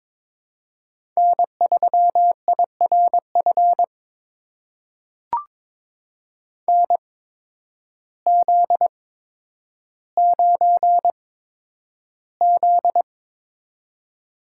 Morse code, heard as a keyed tone, about 22 wpm, spaced as for 10 wpm, 700 Hz. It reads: N3IRF NZ9Z
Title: USA Call Signs - 8x Character Spacing - 22wpm